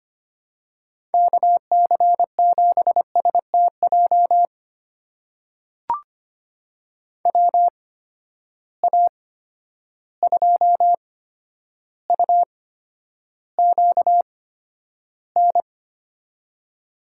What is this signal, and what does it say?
Morse code, heard as a keyed tone, about 25 wpm, spaced as for 11 wpm, 700 Hz. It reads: KC7STJ WA2UQN
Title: USA Call Signs - 6 Characters - 8x Character Spacing - 25wpm